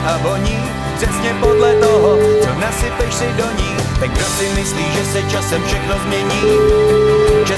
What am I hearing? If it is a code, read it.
Ringback tone (Europe, 425 Hz), with speech.